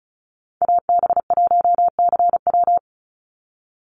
Morse code, heard as a keyed tone, about 34 wpm, 700 Hz. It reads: AB1CW
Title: Call Signs Repeated 3x - 35wpm